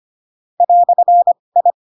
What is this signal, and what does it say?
Morse code, heard as a keyed tone, about 25 wpm, spaced as for 20 wpm, 700 Hz. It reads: "I
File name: Pinocchio / Pinocchio ch2 15 25 ik1hge